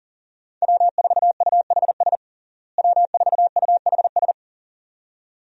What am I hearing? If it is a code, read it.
Morse code, heard as a keyed tone, about 40 wpm, 700 Hz. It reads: W4UHS W4UHS
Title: USA Call Signs - 5 Characters - Repeated 2x - 40wpm